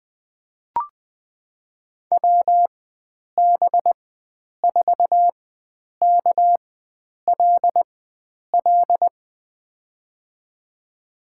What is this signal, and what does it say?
Morse code, heard as a keyed tone, about 20 wpm, spaced as for 9 wpm, 700 Hz. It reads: WB4KLL
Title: USA Call Signs - 6 Characters - 4x Character Spacing - 20wpm